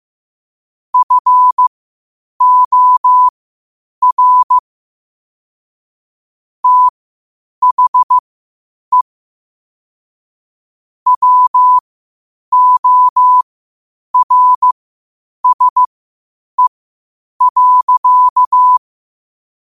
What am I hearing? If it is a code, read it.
Morse code, heard as a keyed tone, about 15 wpm, spaced as for 8 wpm, 1000 Hz. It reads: FOR THE WORSE.
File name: qotdfiles / 8wpm